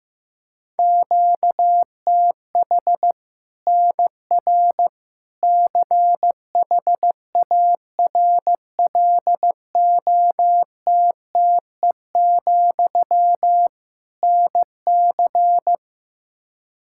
Morse code, heard as a keyed tone, about 16 wpm, 700 Hz. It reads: QTH NR CHARLOTTE, NC
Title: Practice QSO - Exchange 1 - Beginning - 15wpm